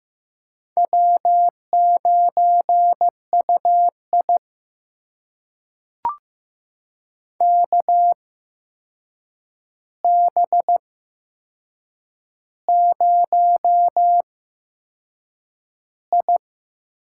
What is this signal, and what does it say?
Morse code, heard as a keyed tone, about 15 wpm, spaced as for 6 wpm, 700 Hz. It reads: W9UI KB0I
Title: USA Call Signs - 8x Character Spacing - 15wpm